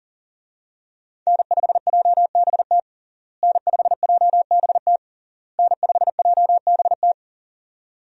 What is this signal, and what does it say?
Morse code, heard as a keyed tone, about 40 wpm, 700 Hz. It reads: N5JBT N5JBT N5JBT
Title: USA Call Signs - 5 Characters - Repeated 3x - 40wpm